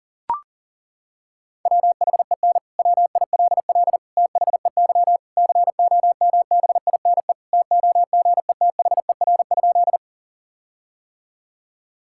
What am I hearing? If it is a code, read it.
Morse code, heard as a keyed tone, about 40 wpm, 700 Hz. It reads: WHEN WILL THEY COMBINE TOGETHER?
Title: Sentences from Top 1600 Words 40wpm